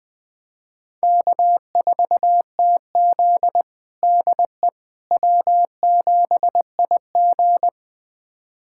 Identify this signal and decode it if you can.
Morse code, heard as a keyed tone, about 21 wpm, 700 Hz. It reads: K4TZ DE W7IG
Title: USA Callsign DE Callsign - 4 Characters - 20wpm